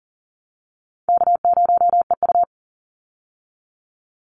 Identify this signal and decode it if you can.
Morse code, heard as a keyed tone, about 40 wpm, 700 Hz. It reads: K0EU
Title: Easy Call Signs Repeated 2x - 40wpm